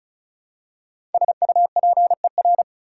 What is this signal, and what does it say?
Morse code, heard as a keyed tone, about 34 wpm, 700 Hz. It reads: SUPER
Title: Prefixes and Suffixes 35wpm